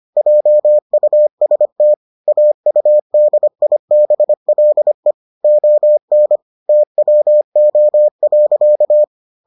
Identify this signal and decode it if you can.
Morse code, heard as a keyed tone, about 25 wpm, 600 Hz. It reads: JUST AUDIBLE ON TWO.